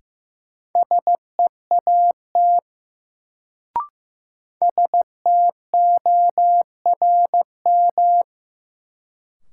Morse code, heard as a keyed tone, about 15 wpm, 700 Hz. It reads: SEAT STORM
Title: Sentences from Top 1500 Words - New Word Drill Alt - 15wpm